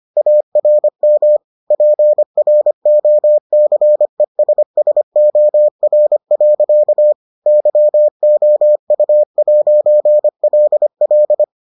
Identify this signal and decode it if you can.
Morse code, heard as a keyed tone, about 25 wpm, 600 Hz. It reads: ARM PROCESSOR. YOU'LL